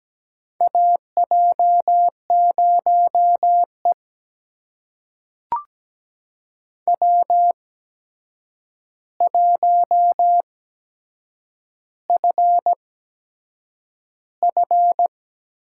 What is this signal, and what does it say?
Morse code, heard as a keyed tone, about 17 wpm, spaced as for 7 wpm, 700 Hz. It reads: AJ0E W1FF
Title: USA Call Signs - 4 Characters - 8x Character Spacing - 17wpm